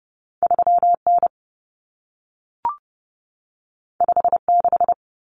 Morse code, heard as a keyed tone, about 30 wpm, 700 Hz. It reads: 3N 56
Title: Common Cut Numbers - 2-Digit 30wpm